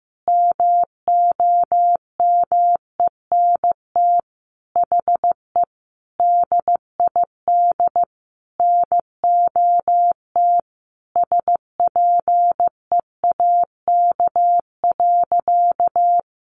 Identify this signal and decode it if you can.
Morse code, heard as a keyed tone, about 15 wpm, 700 Hz. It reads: MOMENT HE DID NOT SPEAK.
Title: Sentences from Top 1100 Words 15wpm